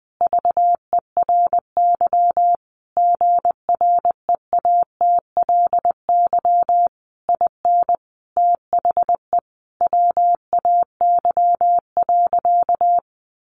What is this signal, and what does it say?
Morse code, heard as a keyed tone, about 20 wpm, 700 Hz. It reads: VERY GREATLY IN THE WAY.
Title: Sentences from Top 1400 Words 20wpm